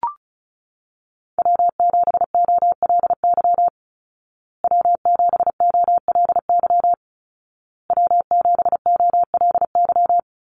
Morse code, heard as a keyed tone, about 34 wpm, 700 Hz. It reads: W7OLY W7OLY W7OLY
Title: Call Signs Repeated 2x - 35wpm